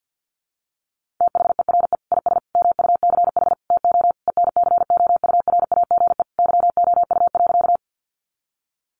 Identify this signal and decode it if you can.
Morse code, heard as a keyed tone, about 48 wpm, 700 Hz. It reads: THERE IS MUCH TO ENCOURAGE YOU.
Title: Sentences from Top 1400 Words 50wpm